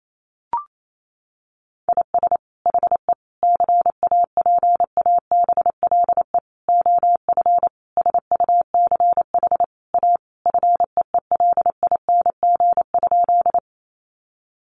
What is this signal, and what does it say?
Morse code, heard as a keyed tone, about 28 wpm, 700 Hz. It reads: IS HE CAPABLE OF SUCH A FEELING?
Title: Sentences from Top 1200 Words 28wpm